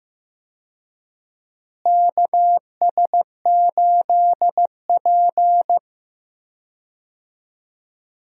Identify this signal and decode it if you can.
Morse code, heard as a keyed tone, about 15 wpm, 700 Hz. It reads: KS8P
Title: USA Call Signs - 4 Characters - Repeated 3x - 15wpm